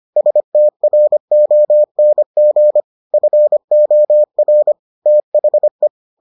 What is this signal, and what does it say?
Morse code, heard as a keyed tone, about 25 wpm, 600 Hz. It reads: STRONG FOR THE